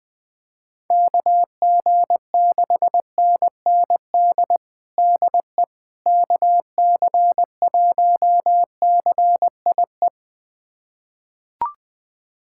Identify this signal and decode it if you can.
Morse code, heard as a keyed tone, about 21 wpm, 700 Hz. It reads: KG6NND DE KC1CIE
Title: USA Callsign DE Callsign - 6 Characters - 20wpm